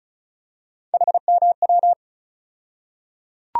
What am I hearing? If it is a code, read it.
Morse code, heard as a keyed tone, about 36 wpm, 700 Hz. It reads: HMW
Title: USA Call Sign Postfixes - 3 Letters 35wpm